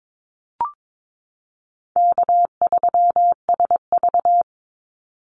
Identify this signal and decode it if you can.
Morse code, heard as a keyed tone, about 22 wpm, 700 Hz. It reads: K3SV
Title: Easy Call Signs Repeated 2x - 22wpm